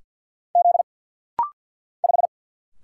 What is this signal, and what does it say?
Morse code, heard as a keyed tone, about 47 wpm, 690 Hz. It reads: Z 5
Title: Single Letter-Number Alt - Rapid-Fire 50wpm